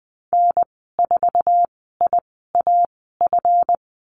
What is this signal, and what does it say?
Morse code, heard as a keyed tone, about 20 wpm, spaced as for 14 wpm, 700 Hz. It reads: N4IAF
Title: 186 Lesson 30g - Letter b - Call Signs 20wpm